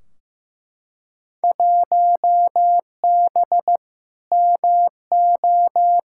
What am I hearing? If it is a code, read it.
Morse code, heard as a keyed tone, about 15 wpm, 700 Hz. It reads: 1B MO
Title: Field Day Exchanges - 15wpm